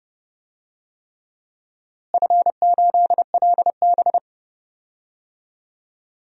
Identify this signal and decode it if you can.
Morse code, heard as a keyed tone, about 30 wpm, 700 Hz. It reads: F8LB
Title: DXCC-100 Call Signs - 4 Characters - Repeated 3x - 30wpm